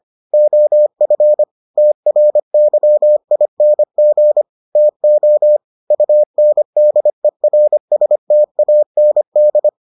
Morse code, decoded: OF TRYING TO UNDERSTAND

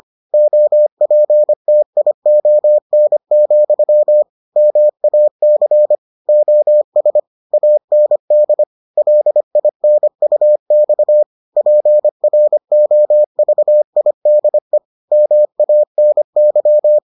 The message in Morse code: OPTION, MAC OS AND LINUX PROVIDE MANY

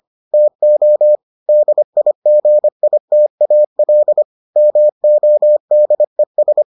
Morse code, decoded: TO DIGITAL MODES

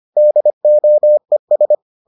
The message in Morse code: DOES

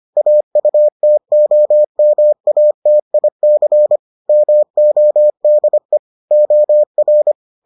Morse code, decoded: AUTOMATIC MODE OR

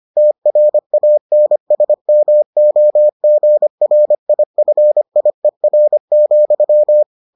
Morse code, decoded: TRANSMOGRIFIER,

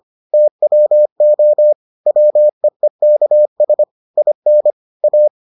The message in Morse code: TWO WEEKS IN A

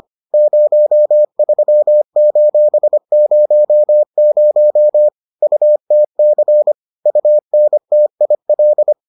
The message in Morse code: 03:00 UTC UNTIL

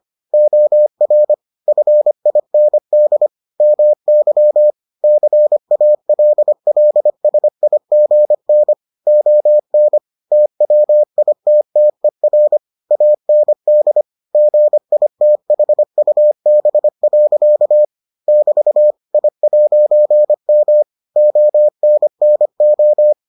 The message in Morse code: OR FIND MY CALLSIGN ON TWITTER AND GITHUB. <BT> I'M ONNO

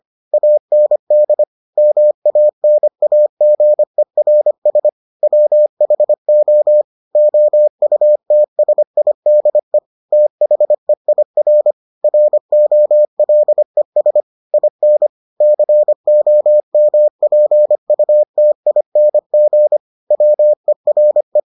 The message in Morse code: AND MANAGERS WHO OUTSIDE THEIR ROLES IN COMPUTING WERE

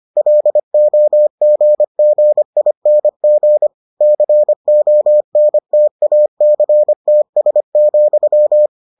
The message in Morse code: LOGGING CONTACTS,

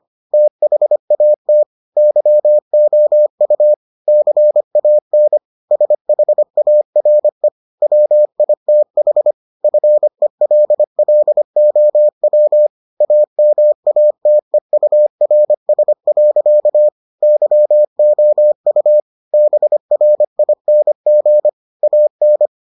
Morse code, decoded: THAT YOU CAN SHARE WITH FELLOW AMATEURS. YOU BRING AN

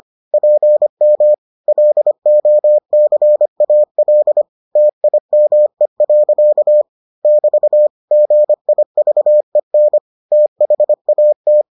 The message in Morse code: PM LOCAL TIME. <BT> GIVEN THAT

